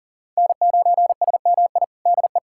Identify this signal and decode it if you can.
Morse code, heard as a keyed tone, about 40 wpm, 700 Hz. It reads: N9SMI DE
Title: USA Callsign DE Callsign - 5 Characters - 40wpm